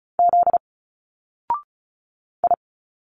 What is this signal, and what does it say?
Morse code, heard as a keyed tone, about 34 wpm, 700 Hz. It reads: Z I
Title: Single Letters 35wpm